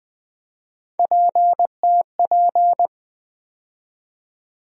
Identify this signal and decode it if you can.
Morse code, heard as a keyed tone, about 21 wpm, 700 Hz. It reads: PTP
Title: USA Call Sign Postfixes - 3 Letters 20wpm